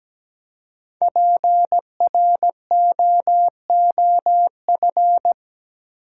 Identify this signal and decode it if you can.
Morse code, heard as a keyed tone, about 17 wpm, 700 Hz. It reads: PROOF
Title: Five Letter Words - 17wpm